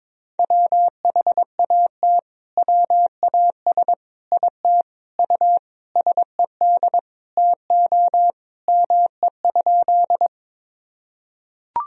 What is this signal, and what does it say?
Morse code, heard as a keyed tone, about 22 wpm, 700 Hz. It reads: WHAT WAS IT U SED TO ME?
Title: Sentences from Top 200 Words and Abbreviations 22wpm